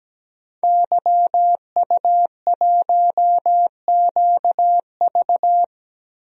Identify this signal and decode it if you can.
Morse code, heard as a keyed tone, about 17 wpm, 700 Hz. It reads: YU1QV
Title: DXCC-100 Call Signs - 5 Characters - Repeated 2x - 17wpm